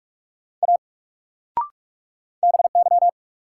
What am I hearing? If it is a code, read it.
Morse code, heard as a keyed tone, about 48 wpm, 700 Hz. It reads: A BY
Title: Sentences from Top 100 Words - New Word Drill - 45wpm